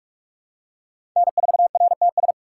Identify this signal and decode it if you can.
Morse code, heard as a keyed tone, about 45 wpm, 700 Hz. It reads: N4RTS
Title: USA Call Signs - 4x Character Spacing - 45wpm